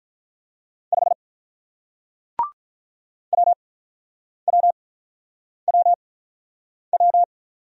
Morse code, heard as a keyed tone, about 46 wpm, spaced as for 9 wpm, 700 Hz. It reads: 5 WWWW